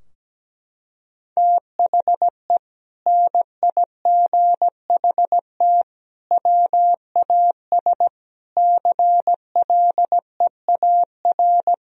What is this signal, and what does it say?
Morse code, heard as a keyed tone, about 17 wpm, 700 Hz. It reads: THE NIGHT WAS CLEAR
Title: Sentences from Top 1000 Words - Review Part 2 - 17wpm